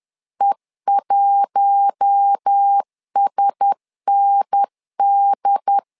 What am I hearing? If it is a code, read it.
Morse code, heard as a keyed tone, about 10 wpm, 800 Hz. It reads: E1SND
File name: LEZIONI / 4 Q1S 10wpm 2133